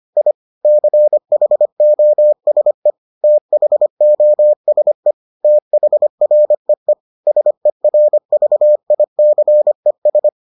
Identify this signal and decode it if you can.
Morse code, heard as a keyed tone, about 25 wpm, 600 Hz. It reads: I CHOSE THOSE THREE SERVICES